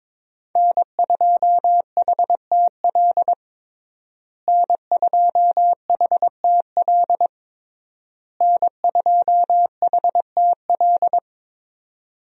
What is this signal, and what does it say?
Morse code, heard as a keyed tone, about 21 wpm, 700 Hz. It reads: N2HTL N2HTL N2HTL
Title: USA Call Signs - 5 Characters - Repeated 3x - 22wpm